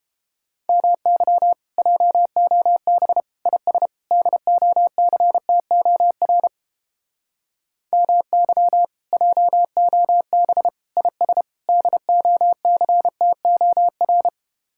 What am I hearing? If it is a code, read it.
Morse code, heard as a keyed tone, about 32 wpm, 700 Hz. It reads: MY JOB IS DOCTOR MY JOB IS DOCTOR